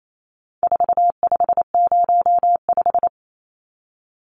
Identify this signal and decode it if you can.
Morse code, heard as a keyed tone, about 28 wpm, 700 Hz. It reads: 4505
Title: Numbers - 4-Digit 28wpm